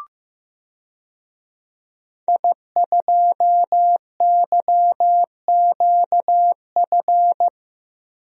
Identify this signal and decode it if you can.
Morse code, heard as a keyed tone, about 15 wpm, 700 Hz. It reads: I2YQF
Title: DXCC-100 Call Signs - 5 Characters - Repeated 3x - 15wpm